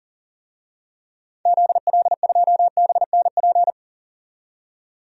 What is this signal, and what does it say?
Morse code, heard as a keyed tone, about 40 wpm, 700 Hz. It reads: ZL2BNP
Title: DXCC-100 Call Signs - 6 Characters - 40wpm